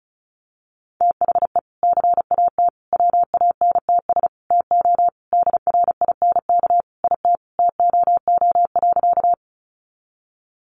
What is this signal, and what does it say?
Morse code, heard as a keyed tone, about 34 wpm, 700 Hz. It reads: THE CAT WANTS TO DRINK IT TOO.
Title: Additional Easy Phrases 35wpm